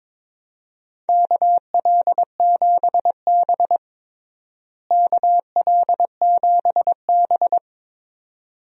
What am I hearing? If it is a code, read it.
Morse code, heard as a keyed tone, about 22 wpm, 700 Hz. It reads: KL7B KL7B
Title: DXCC-100 Call Signs - 4 Characters - Repeated 2x - 22wpm